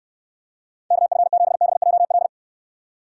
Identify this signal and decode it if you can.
Morse code, heard as a keyed tone, about 68 wpm, 700 Hz. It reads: KF7BJL